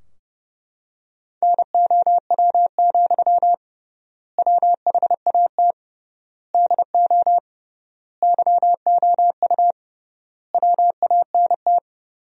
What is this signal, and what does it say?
Morse code, heard as a keyed tone, about 32 wpm, 700 Hz. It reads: NOW, WHAT DO YOU WANT
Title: 3x Word-Spacing - Sentences from Top 200 Words - 30wpm